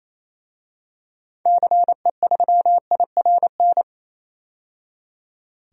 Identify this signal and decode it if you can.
Morse code, heard as a keyed tone, about 28 wpm, 700 Hz. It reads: CE3IRN
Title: DXCC-100 Call Signs - 6 Characters - 28wpm